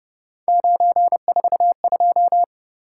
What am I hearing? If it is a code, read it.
Morse code, heard as a keyed tone, about 30 wpm, 700 Hz. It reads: 942